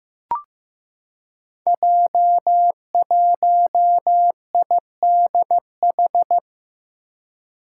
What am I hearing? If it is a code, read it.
Morse code, heard as a keyed tone, about 15 wpm, 700 Hz. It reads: J1IDH
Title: Five Challenging Letter-Number Combinations 15wpm